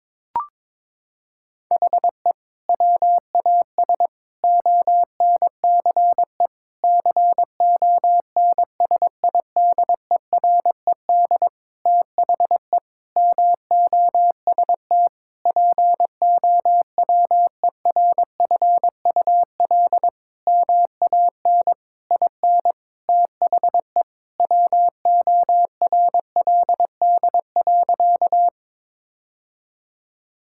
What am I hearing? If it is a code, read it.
Morse code, heard as a keyed tone, about 21 wpm, 700 Hz. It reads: HE WAS ONCE CONSIDERED THE MOST POWERFUL MAN IN THE WORLD.
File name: Sentences with 7-Word Phrases - 22wpm